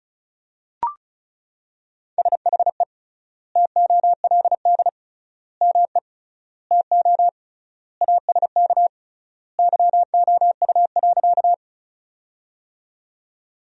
Morse code, heard as a keyed tone, about 36 wpm, 700 Hz. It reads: SHE TOLD ME TO ASK YOU.
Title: 3x Word-Spacing - Sentences from Top 400 Words - 35wpm